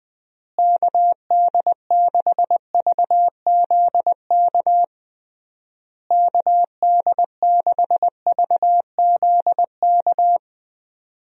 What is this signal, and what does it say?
Morse code, heard as a keyed tone, about 20 wpm, 700 Hz. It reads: KD6VZK KD6VZK
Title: USA Call Signs - 6 Characters - Repeated 3x - 20wpm